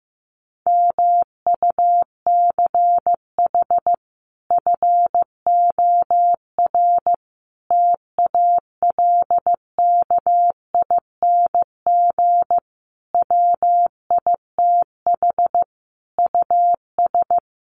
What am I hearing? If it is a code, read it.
Morse code, heard as a keyed tone, about 15 wpm, 700 Hz. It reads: MUCH FOR TALKING WITH US
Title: Sets of 5 Words Encore 15wpm